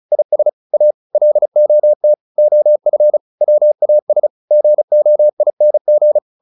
Morse code, heard as a keyed tone, about 34 wpm, 600 Hz. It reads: IS A LOT OF WAS GOING